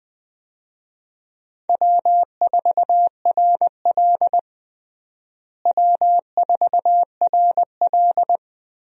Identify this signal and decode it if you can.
Morse code, heard as a keyed tone, about 20 wpm, 700 Hz. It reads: W4RL W4RL
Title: USA Call Signs - 4 Characters - Repeated 2x - 20wpm